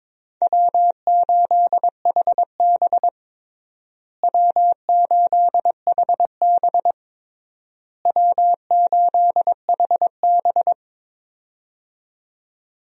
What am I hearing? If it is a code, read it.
Morse code, heard as a keyed tone, about 22 wpm, 700 Hz. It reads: W8HB W8HB W8HB
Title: USA Call Signs - 4 Characters - Repeated 3x - 22wpm